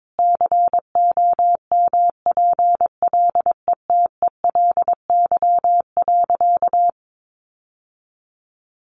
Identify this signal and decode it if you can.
Morse code, heard as a keyed tone, about 22 wpm, 700 Hz. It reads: COMPLETELY.
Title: Sentences from Top 1100 Words 22wpm